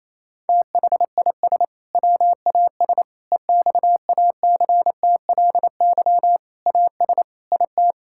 Morse code, heard as a keyed tone, about 28 wpm, 700 Hz. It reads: THIS WAS EXACTLY AS IT
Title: Sentences from Top 1000 Words - Review Part 6 - 28wpm